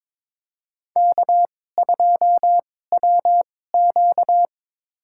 Morse code, heard as a keyed tone, about 23 wpm, spaced as for 16 wpm, 700 Hz. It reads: K2WQ
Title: USA Call Signs - 4 Characters - 2x Character Spacing - 22wpm